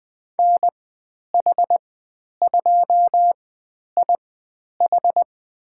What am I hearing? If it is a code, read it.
Morse code, heard as a keyed tone, about 20 wpm, spaced as for 10 wpm, 700 Hz. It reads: NH2IH
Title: 109 Lesson 18g - Letter m - Call Signs 20wpm